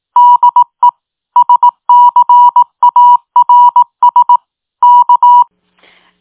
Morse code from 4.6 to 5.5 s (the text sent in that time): K